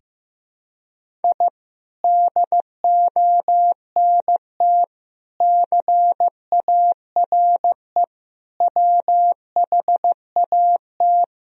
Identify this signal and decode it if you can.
Morse code, heard as a keyed tone, about 15 wpm, 700 Hz. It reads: I DONT CARE WHAT
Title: Sentences from Top 1000 Words - Review Part 1 - 15wpm